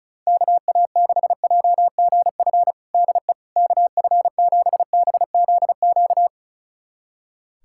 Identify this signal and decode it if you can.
Morse code, heard as a keyed tone, about 34 wpm, 700 Hz. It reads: KA6JGF DE KF7BZQ
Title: USA Callsign DE Callsign - 6 Characters - 35wpm